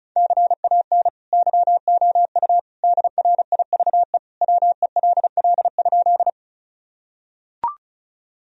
Morse code, heard as a keyed tone, about 36 wpm, 700 Hz. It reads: CAN YOU DRIVE WELL?
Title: Sentences from Top 1000 Words - Review Part 2 - 35wpm